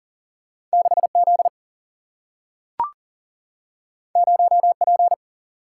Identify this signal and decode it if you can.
Morse code, heard as a keyed tone, about 40 wpm, 700 Hz. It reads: 6Z 0P
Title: Single-Letter Single-Number 40wpm